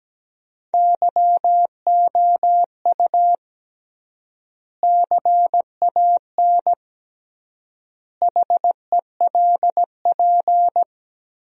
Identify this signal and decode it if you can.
Morse code, heard as a keyed tone, about 17 wpm, 700 Hz. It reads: YOU CAN HELP
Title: 3x Word-Spacing - Sentences from Top 200 Words - 17wpm